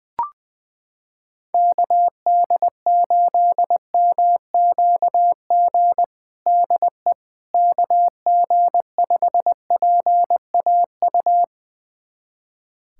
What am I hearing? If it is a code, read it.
Morse code, heard as a keyed tone, about 20 wpm, 700 Hz. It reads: KD8MQG DE KG5PAU
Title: USA Callsign DE Callsign 20wpm